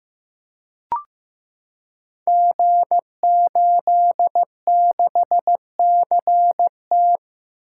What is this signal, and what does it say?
Morse code, heard as a keyed tone, about 16 wpm, 700 Hz. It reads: G86CT
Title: Code Groups - 5 Letter:Numbers - 15wpm